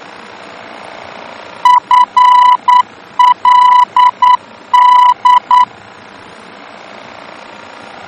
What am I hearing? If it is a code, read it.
Morse code, heard as a keyed tone, about 9 wpm, 990 Hz. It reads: FLD
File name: germany / mx / FLD-D-VOR